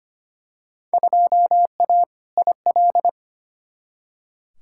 Morse code, heard as a keyed tone, about 25 wpm, 700 Hz. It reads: 2A IL
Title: Field Day Exchanges Encore - 25wpm